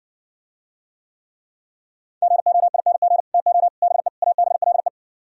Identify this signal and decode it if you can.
Morse code, heard as a keyed tone, about 59 wpm, 700 Hz. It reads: GOING TO BE ABLE